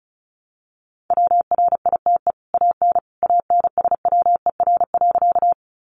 Morse code, heard as a keyed tone, about 34 wpm, 700 Hz. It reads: WRITE AN ANSWER.
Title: Sentences from Top 300 Words 35wpm